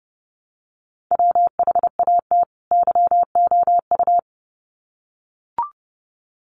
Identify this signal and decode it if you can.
Morse code, heard as a keyed tone, about 30 wpm, 700 Hz. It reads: WHAT YOU
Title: Sets of 2 Words 30wpm